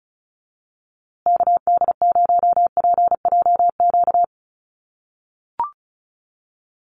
Morse code, heard as a keyed tone, about 34 wpm, 700 Hz. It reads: KD0PJQ
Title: USA Call Signs Repeated 2x 35wpm